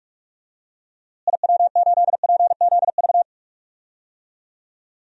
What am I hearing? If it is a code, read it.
Morse code, heard as a keyed tone, about 46 wpm, 700 Hz. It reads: IW8PZV
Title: DXCC-100 Call Signs - 6 Characters - Repeated 3x - 45wpm